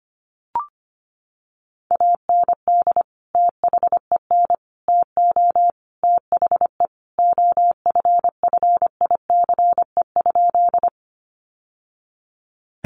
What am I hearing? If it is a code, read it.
Morse code, heard as a keyed tone, about 25 wpm, 700 Hz. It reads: AND THEN TO THE OFFICE?
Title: Sentences from Top 1000 Words 25wpm